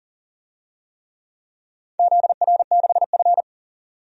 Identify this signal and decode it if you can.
Morse code, heard as a keyed tone, about 40 wpm, 690 Hz. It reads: ZR6F
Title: DXCC-100 Call Signs - 4 Characters - 40wpm